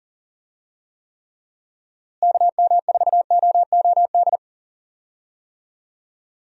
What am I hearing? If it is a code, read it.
Morse code, heard as a keyed tone, about 40 wpm, 700 Hz. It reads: KM4OOD
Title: USA Call Signs - 6 Characters - Repeated 3x - 40wpm